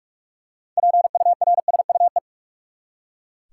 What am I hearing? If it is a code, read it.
Morse code, heard as a keyed tone, about 43 wpm, 700 Hz. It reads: PURSUE
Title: Six Letter Words - 45wpm